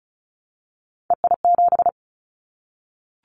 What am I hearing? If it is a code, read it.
Morse code, heard as a keyed tone, about 36 wpm, 690 Hz. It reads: EI7
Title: Callsign Prefixes for Top 50 DXCC - 35wpm